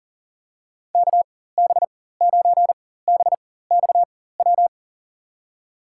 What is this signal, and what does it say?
Morse code, heard as a keyed tone, about 40 wpm, spaced as for 19 wpm, 700 Hz. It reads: KB9BXW
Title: USA Call Signs - 6 Characters - 4x Character Spacing - 40wpm